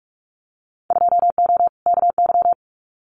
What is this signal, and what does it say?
Morse code, heard as a keyed tone, about 46 wpm, 700 Hz. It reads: 2O KY
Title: Winter Field Day Exchanges 45wpm